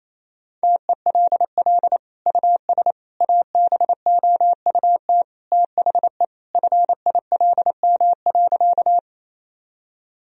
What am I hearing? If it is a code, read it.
Morse code, heard as a keyed tone, about 28 wpm, 700 Hz. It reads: TELL US ABOUT THE FILM.
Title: Sentences from Top 1000 Words - Review Part 3 - 28wpm